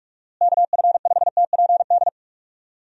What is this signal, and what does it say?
Morse code, heard as a keyed tone, about 46 wpm, 700 Hz. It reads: KF5TPD
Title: USA Call Signs - 6 Characters - Repeated 2x - 45wpm